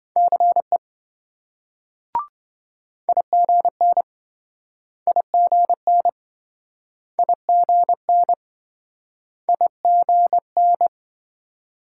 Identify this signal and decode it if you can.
Morse code, heard as a keyed tone, about 27 wpm, 700 Hz. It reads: CE IGN IGN IGN IGN